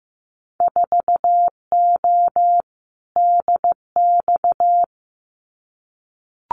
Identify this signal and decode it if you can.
Morse code, heard as a keyed tone, about 15 wpm, 700 Hz. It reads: 4O DX
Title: Winter Field Day Exchanges Encore 15wpm